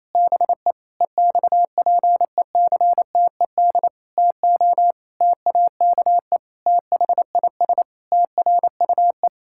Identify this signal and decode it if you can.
Morse code, heard as a keyed tone, about 28 wpm, 700 Hz. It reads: BE EXPECTED TO TAKE THIS TRUE
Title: Pangram Challenge v2 - 28wpm